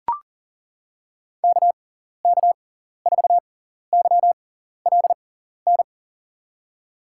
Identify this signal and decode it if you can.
Morse code, heard as a keyed tone, about 40 wpm, spaced as for 14 wpm, 700 Hz. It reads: KK4YLN
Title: USA Call Signs - 6x Character Spacing - 40wpm